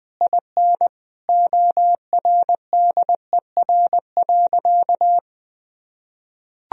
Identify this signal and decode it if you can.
Morse code, heard as a keyed tone, about 20 wpm, 700 Hz. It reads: IN ORDER.
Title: Sentences from Top 1000 Words - Review Part 4 - 20wpm